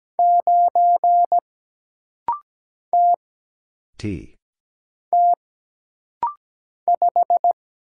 Morse code, heard as a keyed tone, about 17 wpm, 700 Hz. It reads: 9 T T 5
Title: Single Letter-Number Alt - Rapid-Fire 17wpm